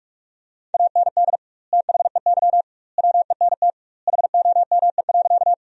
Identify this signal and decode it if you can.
Morse code, heard as a keyed tone, about 46 wpm, 700 Hz. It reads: AND THEY WENT HOME.
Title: 2x Word-Spacing - Sentences from Top 200 Words - 45wpm